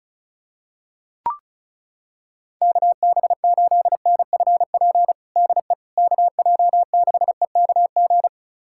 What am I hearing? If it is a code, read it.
Morse code, heard as a keyed tone, about 34 wpm, 700 Hz. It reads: KB8NFP DE KJ6EKG
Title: USA Callsign DE Callsign - 6 Characters - 35wpm